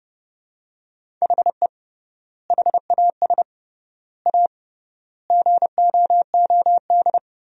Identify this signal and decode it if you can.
Morse code, heard as a keyed tone, about 31 wpm, 700 Hz. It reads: HE HAS A GOOD
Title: 3x Word-Spacing - Sentences from Top 500 Words - 30wpm